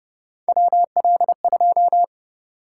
Morse code, heard as a keyed tone, about 29 wpm, 700 Hz. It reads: WL2
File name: Speed-Racing - Three Letter-Number Combinations - 25wpm